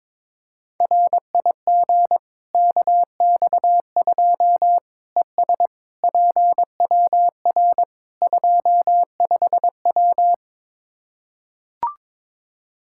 Morse code, heard as a keyed tone, about 21 wpm, 700 Hz. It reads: RIG KX2 ES PWR 25W
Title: Practice QSO - Exchange 2 - Info - 22wpm